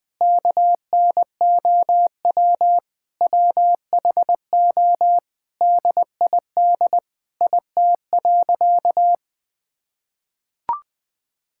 Morse code, heard as a keyed tone, about 21 wpm, 700 Hz. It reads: KNOW WHO DID IT.
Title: Sentences from Top 1000 Words - Review Part 6 - 20wpm